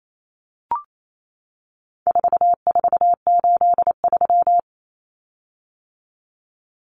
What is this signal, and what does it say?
Morse code, heard as a keyed tone, about 28 wpm, 700 Hz. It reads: 4483
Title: Common Cut Numbers - 4-Digit 28wpm